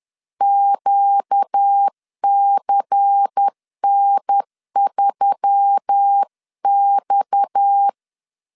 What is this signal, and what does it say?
Morse code, heard as a keyed tone, about 10 wpm, 800 Hz. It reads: QCN3X
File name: CW / DET / 1 6 1500C S B 10wpm 4137